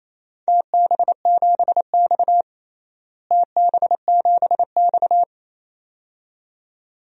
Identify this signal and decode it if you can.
Morse code, heard as a keyed tone, about 28 wpm, 700 Hz. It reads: TB7X TB7X
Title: DXCC-100 Call Signs - 4 Characters - Repeated 2x - 28wpm